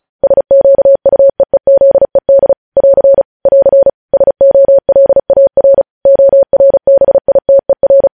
Morse code, heard as a keyed tone, about 36 wpm, 550 Hz. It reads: SQUEEZED <AR> <AR> SOLAR ORBITER
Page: news headlines in morse code at 35 wpm